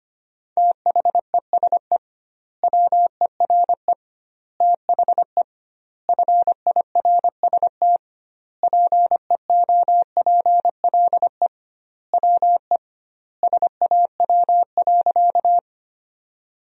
Morse code, heard as a keyed tone, about 25 wpm, 700 Hz. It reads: THESE WERE THE FIRST PEOPLE WE SAW.
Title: 2x Word-Spacing - Sentences from Top 300 Words - 25wpm